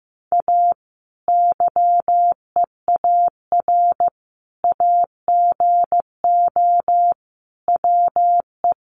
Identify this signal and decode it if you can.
Morse code, heard as a keyed tone, about 15 wpm, 700 Hz. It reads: A YEAR AGO WE
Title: Sentences from Top 1400 Words 15wpm